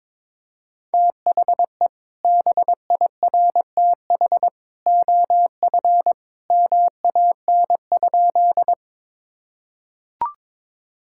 Morse code, heard as a keyed tone, about 21 wpm, 700 Hz. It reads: THE BIRTH OF MAN?
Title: Sentences from Top 1000 Words - Review Part 5 - 22wpm